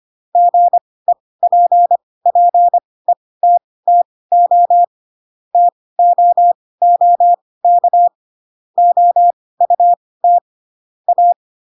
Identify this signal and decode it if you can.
Morse code, heard as a keyed tone, about 25 wpm, spaced as for 18 wpm, 700 Hz. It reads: GEPPETTO TOOK OUT A
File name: Pinocchio / Pinocchio ch7 12 25 ik1hge